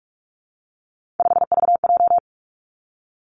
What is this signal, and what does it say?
Morse code, heard as a keyed tone, about 46 wpm, 700 Hz. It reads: 5VJ
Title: Three Challenging Letter-Number Combinations 45wpm